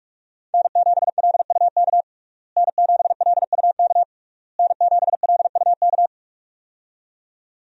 Morse code, heard as a keyed tone, about 46 wpm, 700 Hz. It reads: N7LUK N7LUK N7LUK
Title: USA Call Signs - 5 Characters - Repeated 3x - 45wpm